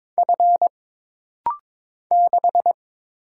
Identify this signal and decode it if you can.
Morse code, heard as a keyed tone, about 22 wpm, 700 Hz. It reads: F 6
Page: This rapid-fire timing is geared to push your ICR (Instant Character Recognition) skills.